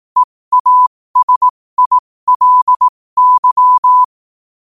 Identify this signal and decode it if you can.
Morse code, heard as a keyed tone, about 18 wpm, spaced as for 15 wpm, 1000 Hz. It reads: EASILY
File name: qotdfiles / 15wpm